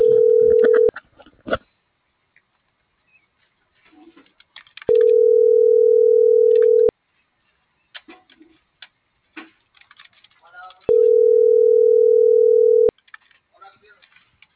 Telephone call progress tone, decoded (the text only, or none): ringback tone